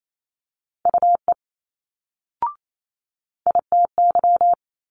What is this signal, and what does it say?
Morse code, heard as a keyed tone, about 28 wpm, 700 Hz. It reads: UE ITY
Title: Multi-letter Phonemes Quick Fire 28wpm